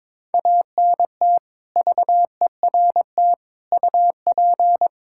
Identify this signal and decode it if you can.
Morse code, heard as a keyed tone, about 22 wpm, 700 Hz. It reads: ANT VERT UP
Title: Practice QSO - Exchange 2 - Info - 22wpm